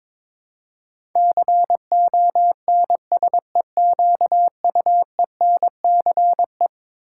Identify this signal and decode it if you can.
Morse code, heard as a keyed tone, about 21 wpm, 700 Hz. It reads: CONSEQUENCE
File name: practice / 22wpm / Eleven Letter Words - 22wpm